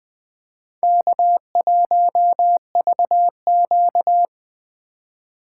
Morse code, heard as a keyed tone, about 21 wpm, 700 Hz. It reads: K1VQ